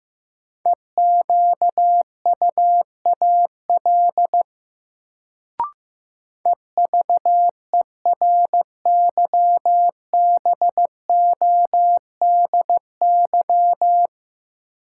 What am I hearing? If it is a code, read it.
Morse code, heard as a keyed tone, about 15 wpm, 700 Hz. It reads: EQUAL EVERYBODY
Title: Sentences from Top 1400 Words - New Word Drill - 15wpm